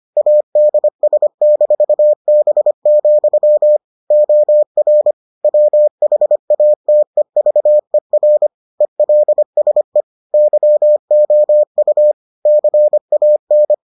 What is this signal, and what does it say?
Morse code, heard as a keyed tone, about 25 wpm, 600 Hz. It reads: ADS-B, OR WHATEVER ELSE YOU CAN